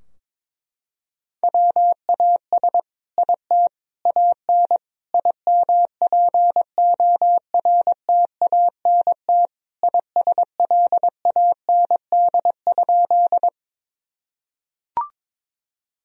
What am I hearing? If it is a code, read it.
Morse code, heard as a keyed tone, about 22 wpm, 700 Hz. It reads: WAS IT AN IMPORTANT ISLAND?
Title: Sentences from Top 1000 Words - Review Part 3 - 22wpm